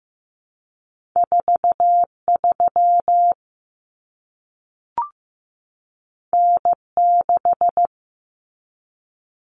Morse code, heard as a keyed tone, about 15 wpm, 700 Hz. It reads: 43 N6
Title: Common Cut Numbers - 2-Digit 15wpm